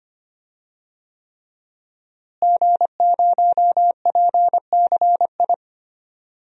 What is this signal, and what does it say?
Morse code, heard as a keyed tone, about 25 wpm, 700 Hz. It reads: G0PCI
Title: DXCC-100 Call Signs - 5 Characters - Repeated 3x - 25wpm